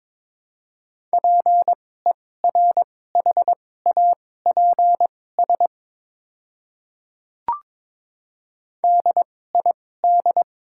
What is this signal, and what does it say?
Morse code, heard as a keyed tone, about 22 wpm, spaced as for 16 wpm, 700 Hz. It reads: PERHAPS DID
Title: 2x Character Spacing - Top 500 Words - 22wpm